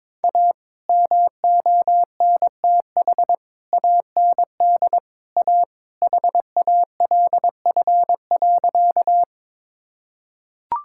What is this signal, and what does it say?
Morse code, heard as a keyed tone, about 22 wpm, 700 Hz. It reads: A MONTH AND A HALF.
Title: Sentences from Top 1000 Words - Review Part 5 - 22wpm